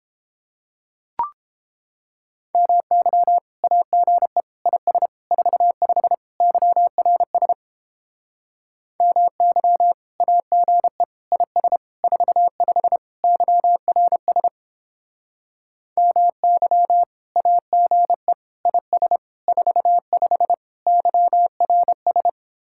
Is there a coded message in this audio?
Yes: Morse code from 2.5 to 22.3 s, heard as a keyed tone, about 29 wpm, 700 Hz. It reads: MY AGE IS 45 YRS MY AGE IS 45 YRS MY AGE IS 45 YRS